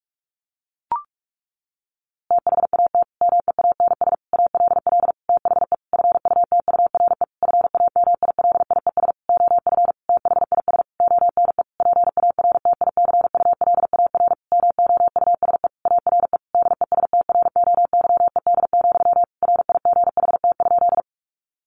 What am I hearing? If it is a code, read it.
Morse code, heard as a keyed tone, about 44 wpm, 700 Hz. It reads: THAT MEANS ALL THE FUTURE FAMILIES OF THIS ONE PARTICULAR MOUSE ARE DESTROYED, RIGHT?
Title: A Sound Of Thunder 45wpm